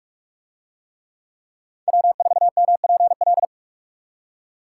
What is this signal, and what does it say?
Morse code, heard as a keyed tone, about 44 wpm, 700 Hz. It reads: W4MPL